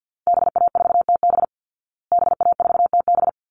Morse code, heard as a keyed tone, about 49 wpm, 700 Hz. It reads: BA4TB BA4TB
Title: Call Signs Repeated 3x - 50wpm